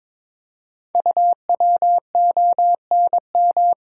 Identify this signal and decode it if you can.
Morse code, heard as a keyed tone, about 22 wpm, 700 Hz. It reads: UWONM
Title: Code Groups - 5 Letters - 22wpm